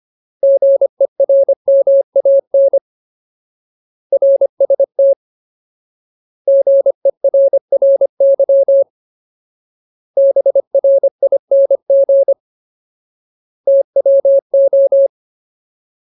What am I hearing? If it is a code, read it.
Morse code, heard as a keyed tone, about 25 wpm, 550 Hz. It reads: GERMAN RST GERRY BRING TWO
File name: Ultimate Word List / 25WPM Ultimate